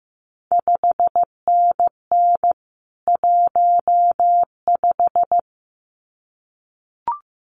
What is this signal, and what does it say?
Morse code, heard as a keyed tone, about 15 wpm, 700 Hz. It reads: 5NN 15
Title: CQ WW DX Contest Exchange - 15wpm